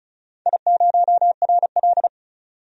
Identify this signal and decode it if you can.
Morse code, heard as a keyed tone, about 36 wpm, 700 Hz. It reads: I0RL
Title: DXCC-100 Call Signs - 4 Characters - 35wpm